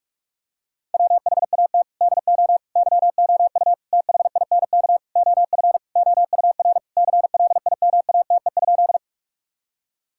Morse code, heard as a keyed tone, about 47 wpm, 700 Hz. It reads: WHAT DO YOU THINK OF OUR CLIMATE?
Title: Sentences from Top 1600 Words 45wpm